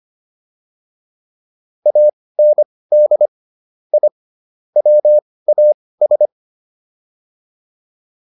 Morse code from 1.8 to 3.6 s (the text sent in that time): AND